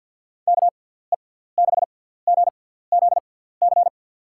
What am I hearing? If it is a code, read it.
Morse code, heard as a keyed tone, about 47 wpm, spaced as for 17 wpm, 700 Hz. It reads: KE6GZC
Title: USA Call Signs - 6 Characters - 6x Character Spacing - 50wpm